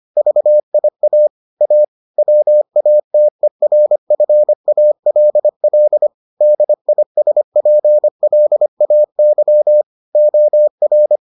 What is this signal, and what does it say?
Morse code, heard as a keyed tone, about 25 wpm, 600 Hz. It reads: VIA A WATERFALL DISPLAY OR